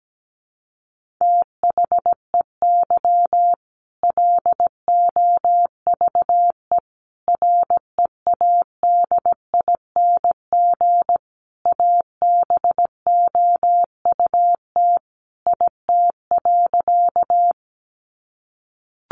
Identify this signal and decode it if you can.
Morse code, heard as a keyed tone, about 17 wpm, 700 Hz. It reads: THEY LOVE READING ABOUT IT.
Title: Sentences from Top 600 Words 17wpm